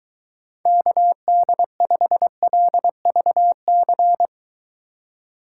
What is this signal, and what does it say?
Morse code, heard as a keyed tone, about 22 wpm, 700 Hz. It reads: KD5LVC